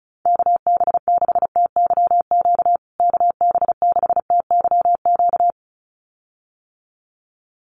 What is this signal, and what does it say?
Morse code, heard as a keyed tone, about 34 wpm, 700 Hz. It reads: KB6TYQ KB6TYQ
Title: USA Call Signs Repeated 2x 35wpm